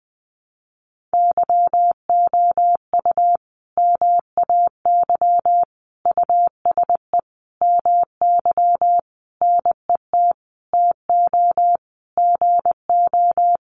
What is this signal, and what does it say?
Morse code, heard as a keyed tone, about 20 wpm, 700 Hz. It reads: YOU MAY USE MY NET TO GO
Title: Sentences from Top 1400 Words 20wpm